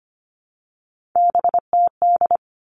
Morse code, heard as a keyed tone, about 25 wpm, 700 Hz. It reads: BTD
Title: All Cut Numbers - 3-Digit 25wpm